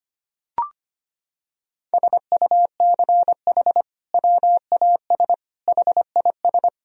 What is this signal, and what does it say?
Morse code, heard as a keyed tone, about 25 wpm, 700 Hz. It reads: SUCH WAS HIS
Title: Sentences from Top 1000 Words - Review Part 4 - 25wpm